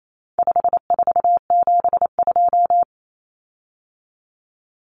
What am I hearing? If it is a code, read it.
Morse code, heard as a keyed tone, about 28 wpm, 700 Hz. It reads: 5472
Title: Common Cut Numbers - 4-Digit 28wpm